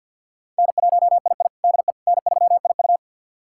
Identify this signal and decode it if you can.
Morse code, heard as a keyed tone, about 51 wpm, 700 Hz. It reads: N1II DE N3IU
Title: USA Callsign DE Callsign - 4 Characters - 50wpm